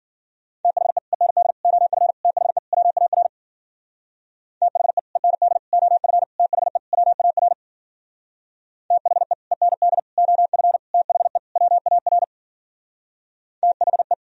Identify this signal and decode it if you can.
Morse code, heard as a keyed tone, about 50 wpm, 700 Hz. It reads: THE END OF THE WAR THE END OF THE WAR THE END OF THE WAR THE